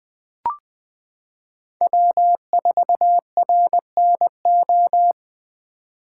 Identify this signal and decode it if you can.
Morse code, heard as a keyed tone, about 20 wpm, 700 Hz. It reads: W4RNO
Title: 2023 POTA Hunter Call Signs - 20wpm